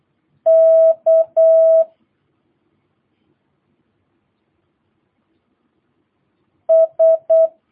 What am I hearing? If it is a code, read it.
Morse code, heard as a keyed tone, about 8 wpm, 650 Hz. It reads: K S
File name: jh8cbh / 2022cw